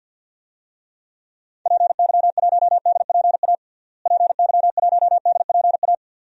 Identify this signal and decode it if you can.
Morse code, heard as a keyed tone, about 53 wpm, 700 Hz. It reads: PY1DPA PY1DPA
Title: DXCC-100 Call Signs - 6 Characters - Repeated 2x - 50wpm